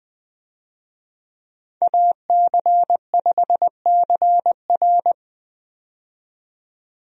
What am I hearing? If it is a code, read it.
Morse code, heard as a keyed tone, about 20 wpm, 700 Hz. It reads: AC5CR
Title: USA Call Signs - 5 Characters - 20wpm